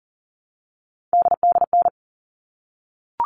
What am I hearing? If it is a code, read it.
Morse code, heard as a keyed tone, about 40 wpm, 700 Hz. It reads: DDN
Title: All Cut Numbers - 3-Digit 40wpm